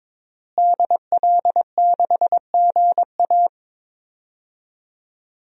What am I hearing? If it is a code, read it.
Morse code, heard as a keyed tone, about 22 wpm, 700 Hz. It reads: DL6GA
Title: DXCC-100 Call Signs - 5 Characters - Repeated 2x - 22wpm